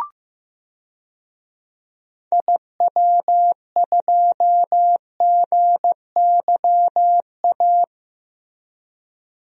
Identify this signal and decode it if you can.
Morse code, heard as a keyed tone, about 15 wpm, 700 Hz. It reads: IW2GYA
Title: DXCC-100 Call Signs - 6 Characters - Repeated 2x - 15wpm